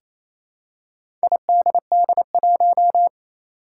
Morse code, heard as a keyed tone, about 27 wpm, 700 Hz. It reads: IDD1
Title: Four Challenging Letter-Number Combinations 28wpm